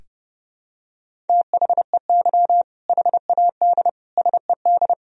Morse code, heard as a keyed tone, about 32 wpm, 700 Hz. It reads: THEY HAD SED
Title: Sentences from Top 400 Words and Abbreviations 30wpm